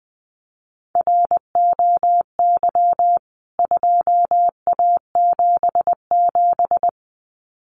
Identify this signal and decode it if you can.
Morse code, heard as a keyed tone, about 19 wpm, 700 Hz. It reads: ROY 2A77
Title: CWT Exchange 20wpm